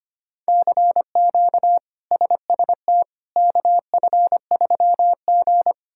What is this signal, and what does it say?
Morse code, heard as a keyed tone, about 25 wpm, 700 Hz. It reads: CQ SST KF3G
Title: K1USN SST QSO Practice - 25wpm